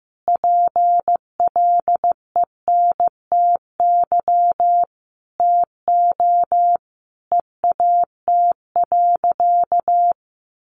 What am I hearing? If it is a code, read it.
Morse code, heard as a keyed tone, about 15 wpm, 700 Hz. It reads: PLENTY TO EAT.
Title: Sentences from Top 1300 Words 15wpm